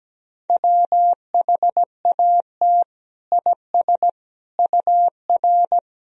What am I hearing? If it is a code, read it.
Morse code, heard as a keyed tone, about 17 wpm, 700 Hz. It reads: WHAT IS UR
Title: Sentences from Top 300 Words and Abbreviations 17wpm